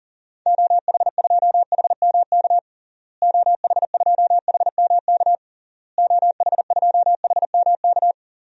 Morse code, heard as a keyed tone, about 37 wpm, 700 Hz. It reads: OH2HMK OH2HMK OH2HMK
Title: DXCC-100 Call Signs - 6 Characters - Repeated 3x - 40wpm